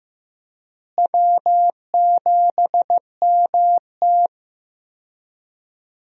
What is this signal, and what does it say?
Morse code, heard as a keyed tone, about 15 wpm, 700 Hz. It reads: W7MT
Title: USA Call Signs - 4 Characters - Repeated 2x - 15wpm